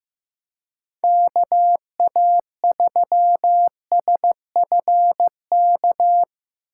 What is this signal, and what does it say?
Morse code, heard as a keyed tone, about 16 wpm, 700 Hz. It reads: KA3SFK
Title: USA Call Signs - 4x Character Spacing - 15wpm